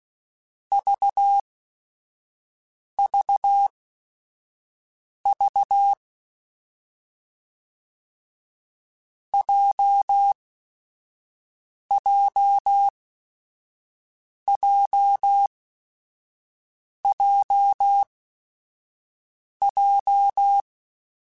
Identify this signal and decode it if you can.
Morse code, heard as a keyed tone, about 16 wpm, spaced as for 5 wpm, 800 Hz. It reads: VVV JJJJJ
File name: Groups@5 / Groups@5 mp3s / JZ17@5